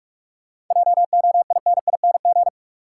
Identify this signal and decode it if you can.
Morse code, heard as a keyed tone, about 48 wpm, 700 Hz. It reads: JOINING
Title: Sets of 5 Words - New Word Drill - 45wpm